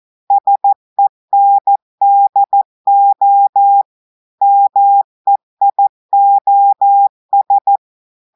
Morse code, heard as a keyed tone, about 14 wpm, 800 Hz. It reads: SENDO MEIOS
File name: audios / portugais / portugais texte en clair 2 vitesse 14 mots minutes